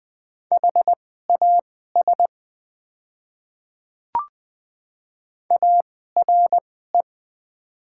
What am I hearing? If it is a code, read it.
Morse code, heard as a keyed tone, about 20 wpm, spaced as for 14 wpm, 700 Hz. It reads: HAS ARE
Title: 2x Character Spacing - Top 100 Words - 20wpm